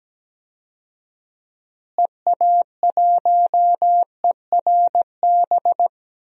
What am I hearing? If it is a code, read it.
Morse code, heard as a keyed tone, about 17 wpm, 700 Hz. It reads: EA1ERB
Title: DXCC-100 Call Signs - 6 Characters - Repeated 3x - 17wpm